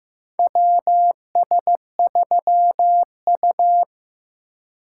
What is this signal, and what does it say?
Morse code, heard as a keyed tone, about 15 wpm, 700 Hz. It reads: WS3U
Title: USA Call Signs - 4 Characters - 3x Character Spacing - 15wpm